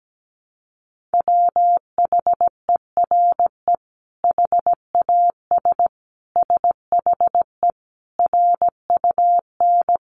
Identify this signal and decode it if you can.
Morse code, heard as a keyed tone, about 17 wpm, 700 Hz. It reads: WHERE HAS SHE RUN
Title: Sentences from Top 300 Words 17wpm